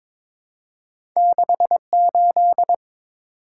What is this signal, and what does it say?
Morse code, heard as a keyed tone, about 22 wpm, 700 Hz. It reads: 68
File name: practice / 22wpm / Two Letter-Number Combinations - 22wpm